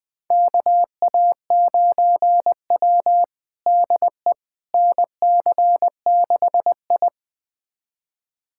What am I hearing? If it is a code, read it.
Morse code, heard as a keyed tone, about 20 wpm, 700 Hz. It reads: KA9W DE NC6I
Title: USA Callsign DE Callsign - 4 Characters - 20wpm